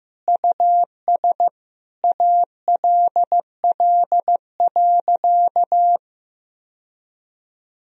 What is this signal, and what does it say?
Morse code, heard as a keyed tone, about 16 wpm, 700 Hz. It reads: US ALL.
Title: Sentences from Top 1000 Words - Review Part 1 - 15wpm